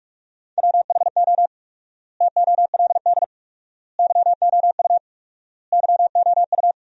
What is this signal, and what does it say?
Morse code, heard as a keyed tone, about 41 wpm, 700 Hz. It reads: WHO TOLD YOU YOU
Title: 4x Word-Spacing - Sentences from Top 400 Words - 45wpm